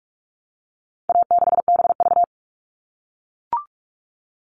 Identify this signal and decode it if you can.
Morse code, heard as a keyed tone, about 44 wpm, 700 Hz. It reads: A6BV
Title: All Cut Numbers - 4-Digit 45wpm